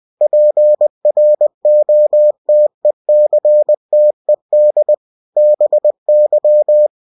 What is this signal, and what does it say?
Morse code, heard as a keyed tone, about 20 wpm, 600 Hz. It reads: PROTECTED BY
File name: cw-books / winnie-the-pooh / winnie-20wpm-page-0112